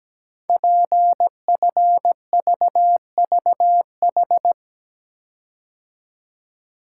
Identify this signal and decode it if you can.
Morse code, heard as a keyed tone, about 17 wpm, 700 Hz. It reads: PFVVH
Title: Code Groups - 5 Letters - 17wpm